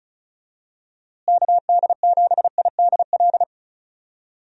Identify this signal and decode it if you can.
Morse code, heard as a keyed tone, about 34 wpm, 700 Hz. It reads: KD7IDL
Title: USA Call Signs - 6 Characters - Repeated 2x - 35wpm